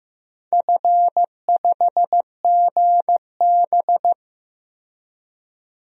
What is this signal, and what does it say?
Morse code, heard as a keyed tone, about 15 wpm, 700 Hz. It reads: F5GB
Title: DXCC-100 Call Signs - 4 Characters - 15wpm